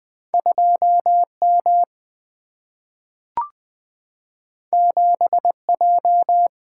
Morse code, heard as a keyed tone, about 20 wpm, 700 Hz. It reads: 2 M 7 J
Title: Single-Letter Single-Number 20wpm